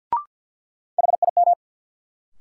Morse code, heard as a keyed tone, about 53 wpm, 700 Hz. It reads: HIM
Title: Sentences from Top 100 Words - New Word Drill - 50wpm